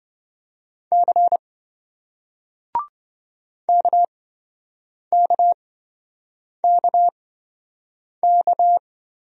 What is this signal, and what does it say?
Morse code, heard as a keyed tone, about 28 wpm, spaced as for 7 wpm, 700 Hz. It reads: C KKKK